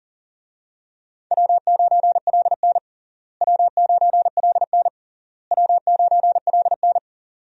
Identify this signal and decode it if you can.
Morse code, heard as a keyed tone, about 37 wpm, 700 Hz. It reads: W9LN W9LN W9LN
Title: USA Call Signs - 4 Characters - Repeated 3x - 40wpm